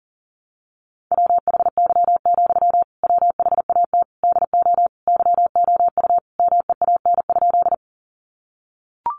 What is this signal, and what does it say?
Morse code, heard as a keyed tone, about 40 wpm, 700 Hz. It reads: WHY, WHAT DO YOU MEAN?
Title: Sentences from Top 200 Words 40wpm